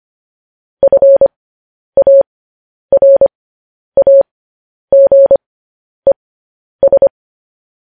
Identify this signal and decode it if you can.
Morse code, heard as a keyed tone, about 25 wpm, spaced as for 10 wpm, 550 Hz. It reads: FARAGES
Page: news-headlines-in-morse-code-at-10-wpm